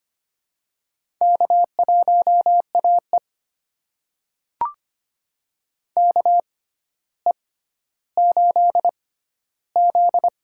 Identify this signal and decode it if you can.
Morse code, heard as a keyed tone, about 25 wpm, spaced as for 13 wpm, 700 Hz. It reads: K1AE KE8Z
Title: USA Call Signs - 4 Characters - 6x Character Spacing - 25wpm